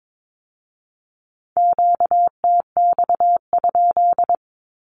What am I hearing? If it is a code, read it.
Morse code, heard as a keyed tone, about 22 wpm, 700 Hz. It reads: QTX?
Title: Q-Codes with Spoken Meaning - 22wpm